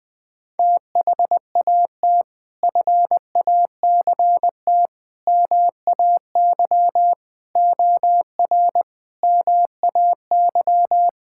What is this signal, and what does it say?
Morse code, heard as a keyed tone, about 21 wpm, 700 Hz. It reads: THAT FACT MAY OR MAY
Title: Sentences from Top 1000 Words - Review Part 6 - 20wpm